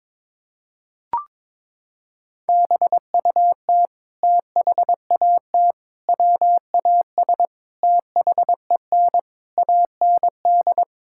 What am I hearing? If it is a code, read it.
Morse code, heard as a keyed tone, about 22 wpm, 700 Hz. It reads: BUT THAT WAS THEN AND
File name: practice / 22wpm / Sets of 8 Words - 22wpm